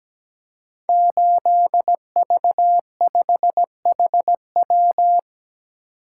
Morse code, heard as a keyed tone, about 17 wpm, 700 Hz. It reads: 8V5HW
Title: Code Groups - 5 Letter:Numbers - 17wpm